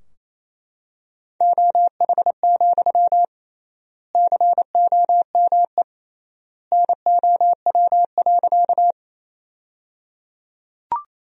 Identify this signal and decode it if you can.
Morse code, heard as a keyed tone, about 27 wpm, 700 Hz. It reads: OH, COME NOW.
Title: 3x Word-Spacing - Sentences from Top 500 Words - 28wpm